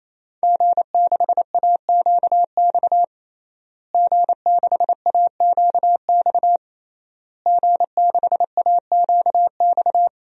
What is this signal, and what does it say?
Morse code, heard as a keyed tone, about 28 wpm, 700 Hz. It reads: G6AQX G6AQX G6AQX
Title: DXCC-100 Call Signs - 5 Characters - Repeated 3x - 28wpm